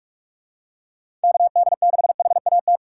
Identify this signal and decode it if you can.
Morse code, heard as a keyed tone, about 42 wpm, 700 Hz. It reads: KD6HAT
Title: USA Call Signs - 2x Character Spacing - 45wpm